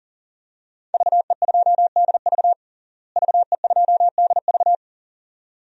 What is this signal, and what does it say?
Morse code, heard as a keyed tone, about 40 wpm, 690 Hz. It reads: VE2DV VE2DV
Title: DXCC-100 Call Signs - 5 Characters - Repeated 2x - 40wpm